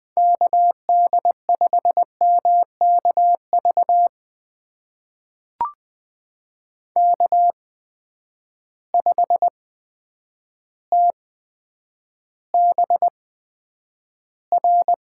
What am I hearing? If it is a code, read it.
Morse code, heard as a keyed tone, about 20 wpm, spaced as for 9 wpm, 700 Hz. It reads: KD5MKV K5TBR
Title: USA Call Signs - 8x Character Spacing - 20wpm